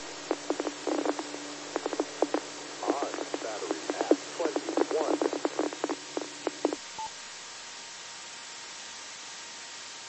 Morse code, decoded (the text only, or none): none